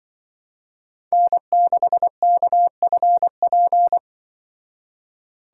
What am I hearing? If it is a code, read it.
Morse code, heard as a keyed tone, about 24 wpm, 700 Hz. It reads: N6KFP
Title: Speed Racing - USA Call Signs 20wpm